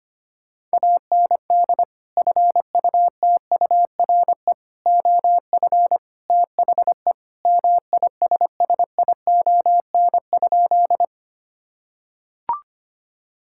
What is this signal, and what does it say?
Morse code, heard as a keyed tone, about 25 wpm, 700 Hz. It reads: AND FUTURE OF THE MISSION?
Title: Sentences from Top 1600 Words 25wpm